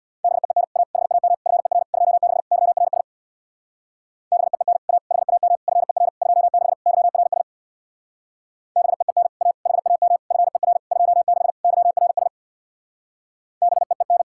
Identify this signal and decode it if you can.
Morse code, heard as a keyed tone, about 64 wpm, 700 Hz. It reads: BEEN A HAM FER 26 YRS BEEN A HAM FER 26 YRS BEEN A HAM FER 26 YRS BEEN